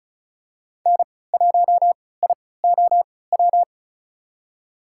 Morse code, heard as a keyed tone, about 36 wpm, spaced as for 20 wpm, 700 Hz. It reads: N1IOW
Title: USA Call Signs - 5 Characters - 3x Character Spacing - 35wpm